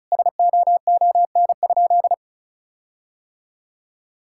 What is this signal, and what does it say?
Morse code, heard as a keyed tone, about 36 wpm, 700 Hz. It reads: SOON?
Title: Sentences from Top 1600 Words 35wpm